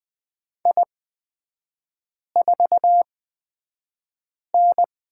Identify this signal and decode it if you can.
Morse code, heard as a keyed tone, about 20 wpm, spaced as for 5 wpm, 700 Hz. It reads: I4N
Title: 243 Lesson 39e - Letter z - Three Characters 20wpm